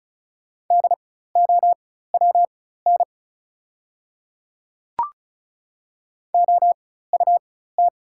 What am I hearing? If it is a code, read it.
Morse code, heard as a keyed tone, about 34 wpm, spaced as for 16 wpm, 700 Hz. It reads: DOWN OUT
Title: 4x Character Spacing - Top 100 Words - 35wpm